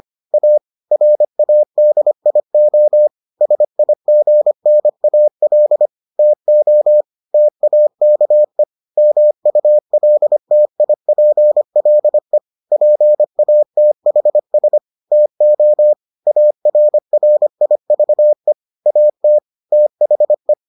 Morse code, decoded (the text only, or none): A RADIO SIGNAL TO TAKE MULTIPLE PATHS TO ARRIVE AT THE